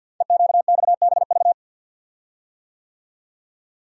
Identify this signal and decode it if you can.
Morse code, heard as a keyed tone, about 50 wpm, 700 Hz. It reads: EQXBV
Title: Code Groups - 5 Letter:Numbers - 50wpm